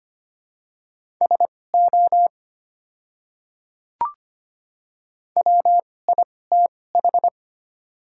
Morse code, heard as a keyed tone, about 25 wpm, spaced as for 18 wpm, 700 Hz. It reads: SO WITH
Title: 2x Character Spacing - Top 300 Words - 25wpm